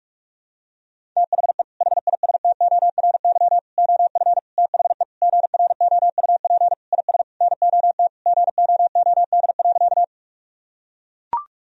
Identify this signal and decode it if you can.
Morse code, heard as a keyed tone, about 43 wpm, 700 Hz. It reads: THE HISTORY OF THE GROUP IS NOT GOOD.
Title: Sentences from Top 1000 Words - Review Part 1 - 45wpm